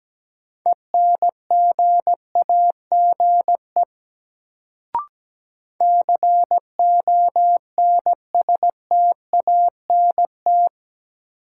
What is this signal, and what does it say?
Morse code, heard as a keyed tone, about 17 wpm, 700 Hz. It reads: ENGAGE CONSTANT
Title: Sentences from Top 1500 Words - New Word Drill - 17wpm